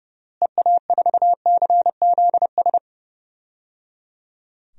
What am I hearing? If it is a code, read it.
Morse code, heard as a keyed tone, about 30 wpm, 700 Hz. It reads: EA4CZS
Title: DXCC-100 Call Signs - 6 Characters - Repeated 3x - 30wpm